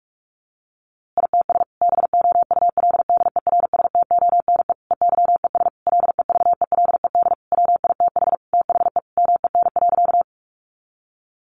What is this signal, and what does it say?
Morse code, heard as a keyed tone, about 44 wpm, 700 Hz. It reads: ITS BOULDERSTONE EYES LEVELED WITH THE MEN.
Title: A Sound Of Thunder 45wpm